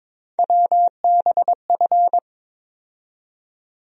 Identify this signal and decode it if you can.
Morse code, heard as a keyed tone, about 22 wpm, 700 Hz. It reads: WBF
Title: USA Call Sign Postfixes - 3 Letters 22wpm